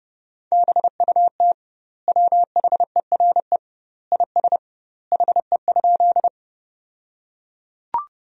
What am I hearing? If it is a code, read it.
Morse code, heard as a keyed tone, about 30 wpm, 700 Hz. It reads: BUT WHERE IS HE?
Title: 2x Word-Spacing - Sentences from Top 200 Words - 30wpm